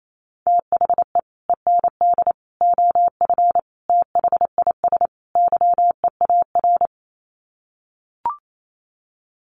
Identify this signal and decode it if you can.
Morse code, heard as a keyed tone, about 28 wpm, 700 Hz. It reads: THE END OF THIS YEAR
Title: Sets of 5 Words Encore 28wpm